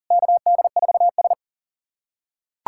Morse code, heard as a keyed tone, about 40 wpm, 700 Hz. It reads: KD4S
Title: USA Callsign DE Callsign - 4 Characters - 40wpm